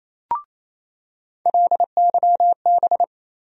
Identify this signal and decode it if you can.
Morse code, heard as a keyed tone, about 28 wpm, 700 Hz. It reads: LYB
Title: USA Call Sign Postfixes - 3 Letters 28wpm